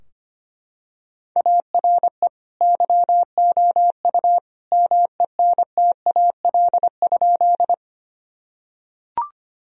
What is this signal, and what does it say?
Morse code, heard as a keyed tone, about 25 wpm, 700 Hz. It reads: ARE YOU MENTAL?
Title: Sentences from Top 1000 Words - Review Part 4 - 25wpm